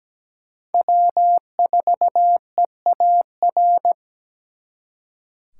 Morse code, heard as a keyed tone, about 20 wpm, 700 Hz. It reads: W4EAR
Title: Speed Racing - USA 1x3 Call Signs 17wpm